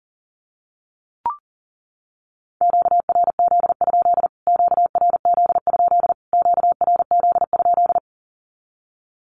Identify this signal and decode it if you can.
Morse code, heard as a keyed tone, about 40 wpm, 700 Hz. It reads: QRZ? QRZ? QRZ?
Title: Q-Codes Repeated 3x 40wpm